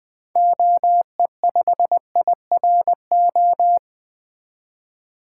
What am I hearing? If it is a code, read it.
Morse code, heard as a keyed tone, about 20 wpm, 700 Hz. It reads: OE5IRO
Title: 2023 SOTA Activator Call Signs - 20wpm